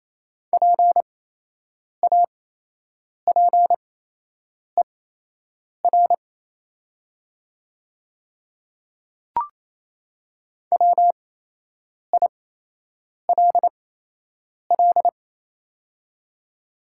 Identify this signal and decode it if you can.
Morse code, heard as a keyed tone, about 28 wpm, spaced as for 8 wpm, 700 Hz. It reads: PAPER WILL
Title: 8x Character Spacing - Top 500 Words - 28wpm